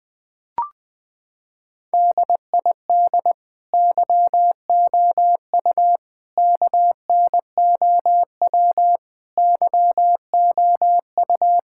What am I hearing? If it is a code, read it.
Morse code, heard as a keyed tone, about 20 wpm, 700 Hz. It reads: DID YOU KNOW YOU
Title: Sentences with 9-Word Phrases - 20wpm